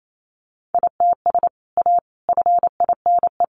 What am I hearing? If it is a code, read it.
Morse code, heard as a keyed tone, about 28 wpm, 700 Hz. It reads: ITS A FINE
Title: Sentences from Top 600 Words 28wpm